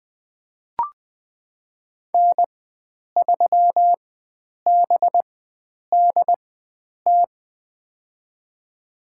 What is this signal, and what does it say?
Morse code, heard as a keyed tone, about 20 wpm, spaced as for 9 wpm, 700 Hz. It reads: N3BDT
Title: USA Call Signs - 5 Characters - 4x Character Spacing - 20wpm